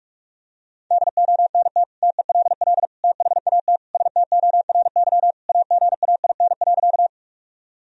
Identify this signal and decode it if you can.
Morse code, heard as a keyed tone, about 46 wpm, 700 Hz. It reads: DONT TELL THAT STORY AGAIN.
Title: Sentences from Top 1000 Words - Review Part 1 - 45wpm